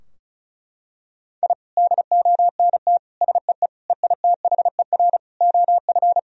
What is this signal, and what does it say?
Morse code, heard as a keyed tone, about 34 wpm, 700 Hz. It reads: I DONT SEE EITHER OF
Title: Sentences from Top 1000 Words - Review Part 4 - 35wpm